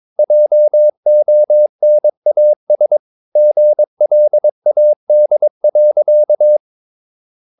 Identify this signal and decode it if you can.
Morse code, heard as a keyed tone, about 22 wpm, 600 Hz. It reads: JONAS GLAD.
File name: Ferie QRS 22wpm